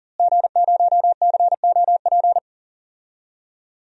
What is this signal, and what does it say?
Morse code, heard as a keyed tone, about 40 wpm, 700 Hz. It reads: G0COP
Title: DXCC-100 Call Signs - 5 Characters - Repeated 3x - 40wpm